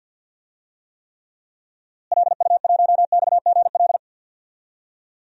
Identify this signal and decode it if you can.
Morse code, heard as a keyed tone, about 47 wpm, 700 Hz. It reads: LU1XGL
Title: DXCC-100 Call Signs - 6 Characters - Repeated 3x - 50wpm